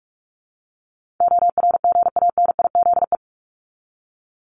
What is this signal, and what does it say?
Morse code, heard as a keyed tone, about 44 wpm, 700 Hz. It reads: ORGANIZE
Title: Top 1751-2000 Words 45wpm